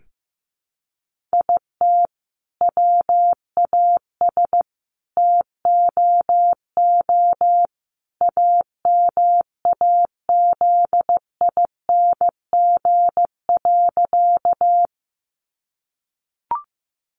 Morse code, heard as a keyed tone, about 15 wpm, 700 Hz. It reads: IT WAS TOO AMAZING.
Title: Sentences from Top 1500 Words 15wpm